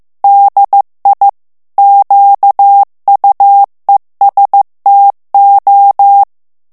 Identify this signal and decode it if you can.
Morse code, heard as a keyed tone, about 15 wpm, 800 Hz. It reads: DI QUESTO
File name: ITALIEN texte clair 1 vitesse 14 mots minute